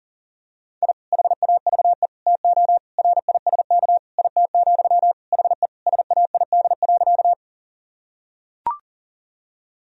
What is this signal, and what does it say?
Morse code, heard as a keyed tone, about 40 wpm, 700 Hz. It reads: I HAVE TO RISK IT, HE SAID.
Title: Sentences from Top 1000 Words - Review Part 4 - 40wpm